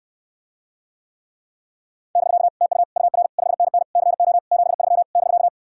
Morse code, heard as a keyed tone, about 66 wpm, 700 Hz. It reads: <BK> TU UR 5NN CO 73 <BK>